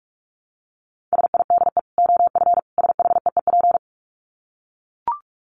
Morse code, heard as a keyed tone, about 44 wpm, 700 Hz. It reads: SIDE OF SHEEP
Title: Dit Challenge 45wpm